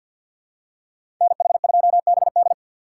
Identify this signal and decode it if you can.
Morse code, heard as a keyed tone, about 52 wpm, 700 Hz. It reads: NH2BD
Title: USA Call Signs - 5 Characters - 3x Character Spacing - 50wpm